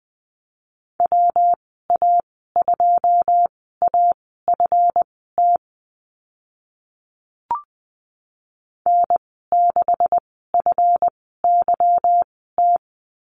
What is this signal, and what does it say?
Morse code, heard as a keyed tone, about 20 wpm, spaced as for 14 wpm, 700 Hz. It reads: WA2AFT N6FYT
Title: 147 Lesson 24g - Letter y - Call Signs 20wpm